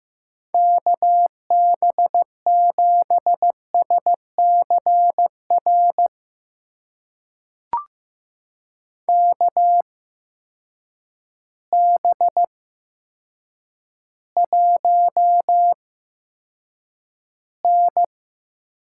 Morse code, heard as a keyed tone, about 15 wpm, spaced as for 8 wpm, 700 Hz. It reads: KB7SCR KB1N
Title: USA Call Signs - 6 Characters - 8x Character Spacing - 15wpm